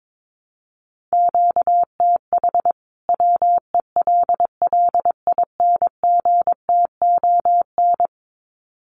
Morse code, heard as a keyed tone, about 22 wpm, 700 Hz. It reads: QTH WELLINGTON
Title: Common QSO Phrases 22wpm